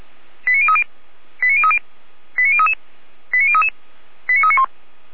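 Five-tone selective call (ZVEI), heard as quick runs of five tones, tones of about 70 ms. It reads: 89030 89030 89033 89033 89391
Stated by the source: ZVEI five-tone call received on an SDR.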